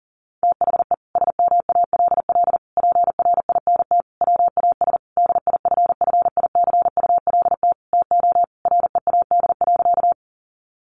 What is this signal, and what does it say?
Morse code, heard as a keyed tone, about 40 wpm, 700 Hz. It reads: THE SMALL PRINT WAS DIFFICULT TO READ.
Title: Sentences from Top 1400 Words 40wpm